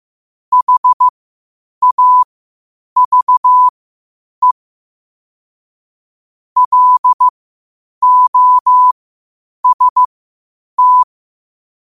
Morse code, heard as a keyed tone, about 15 wpm, spaced as for 8 wpm, 1000 Hz. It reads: HAVE LOST
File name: qotdfiles / 8wpm